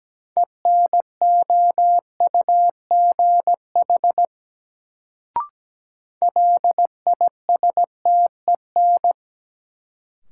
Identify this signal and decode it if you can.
Morse code, heard as a keyed tone, about 17 wpm, 700 Hz. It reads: ENOUGH LISTEN
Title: Sentences from Top 400 Words - New Word Drill - 17wpm